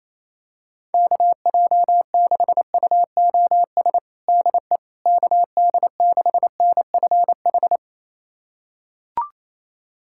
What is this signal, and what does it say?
Morse code, heard as a keyed tone, about 28 wpm, 700 Hz. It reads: KJ6UOS DE KD6NFH
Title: USA Callsign DE Callsign - 6 Characters - 28wpm